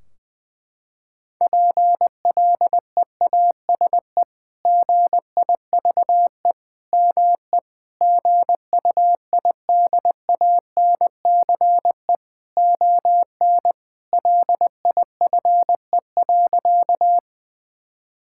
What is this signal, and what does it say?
Morse code, heard as a keyed tone, about 20 wpm, 700 Hz. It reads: PLEASE GIVE ME GUIDANCE ON LIFE.
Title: Sentences from Top 1600 Words 20wpm